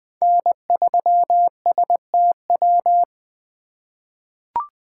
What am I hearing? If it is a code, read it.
Morse code, heard as a keyed tone, about 20 wpm, 700 Hz. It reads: N3STW
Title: USA Call Signs - 5 Characters - 2x Character Spacing - 20wpm